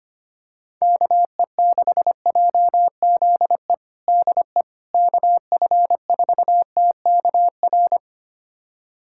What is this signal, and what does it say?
Morse code, heard as a keyed tone, about 25 wpm, 700 Hz. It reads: KE6JZE DE KF4TKR
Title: USA Callsign DE Callsign - 6 Characters - 25wpm